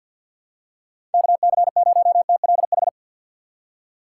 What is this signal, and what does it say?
Morse code, heard as a keyed tone, about 53 wpm, 700 Hz. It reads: KC0TLH
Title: USA Call Signs - 3x Character Spacing - 50wpm